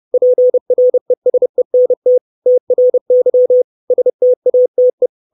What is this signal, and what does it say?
Morse code, heard as a keyed tone, about 30 wpm, 500 Hz. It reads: PRESENT TRY STATE
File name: 632 Words 30WPM